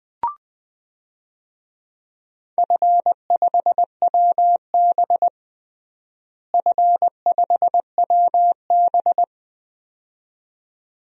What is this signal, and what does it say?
Morse code, heard as a keyed tone, about 20 wpm, 700 Hz. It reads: F5WB F5WB
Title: DXCC-100 Call Signs - 4 Characters - Repeated 2x - 20wpm